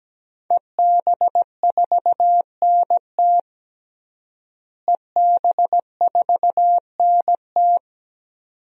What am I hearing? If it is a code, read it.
Morse code, heard as a keyed tone, about 17 wpm, 700 Hz. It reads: EB4NT EB4NT
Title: DXCC-100 Call Signs - 5 Characters - Repeated 2x - 17wpm